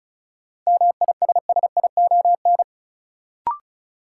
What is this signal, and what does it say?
Morse code, heard as a keyed tone, about 36 wpm, 700 Hz. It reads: MISSION
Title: Sentences from Top 1600 Words - New Word Drill Alt - 35wpm